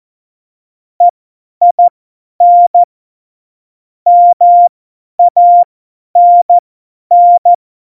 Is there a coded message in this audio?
Yes: Morse code from 1.0 to 7.5 s, heard as a keyed tone, about 14 wpm, spaced as for 10 wpm, 700 Hz. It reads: EIN MANN